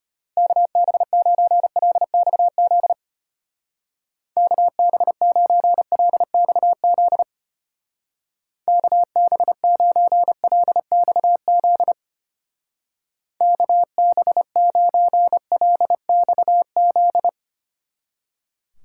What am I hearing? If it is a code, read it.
Morse code, heard as a keyed tone, about 31 wpm, 700 Hz. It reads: KB9LXZ KB9LXZ KB9LXZ KB9LXZ
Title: Speed Racing - USA Call Signs 25wpm